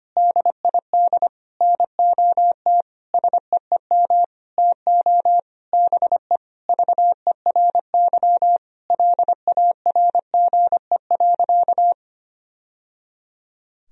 Morse code, heard as a keyed tone, about 25 wpm, 700 Hz. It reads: DID NOT SEEM TO BE VERY LARGE.
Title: Sentences from Top 1000 Words - Review Part 1 - 25wpm